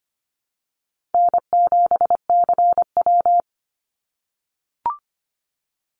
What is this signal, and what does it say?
Morse code, heard as a keyed tone, about 25 wpm, 700 Hz. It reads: N7CW
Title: NA SOTA Activator Callsigns 25wpm